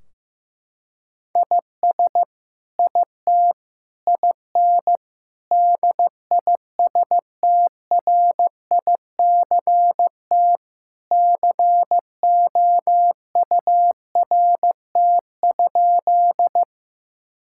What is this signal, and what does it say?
Morse code, heard as a keyed tone, about 15 wpm, 700 Hz. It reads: IS IT IN DISTRICT COURT?
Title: Sentences from Top 1600 Words 15wpm